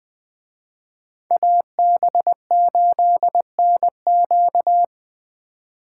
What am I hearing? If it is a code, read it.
Morse code, heard as a keyed tone, about 20 wpm, 700 Hz. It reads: AB8NQ
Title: Speed Racing - USA 2x2 Call Signs 17wpm